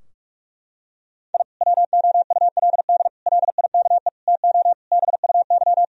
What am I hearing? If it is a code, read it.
Morse code, heard as a keyed tone, about 42 wpm, 700 Hz. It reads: I WOULD LIKE TO BUY
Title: Sentences from Top 1000 Words - Review Part 4 - 45wpm